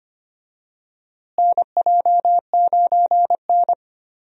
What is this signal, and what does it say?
Morse code, heard as a keyed tone, about 25 wpm, 700 Hz. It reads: NJ9N
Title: USA Call Signs - 4 Characters - Repeated 2x - 25wpm